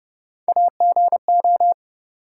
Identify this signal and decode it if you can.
Morse code, heard as a keyed tone, about 30 wpm, 700 Hz. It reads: AGO
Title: Three Letter Words - 30wpm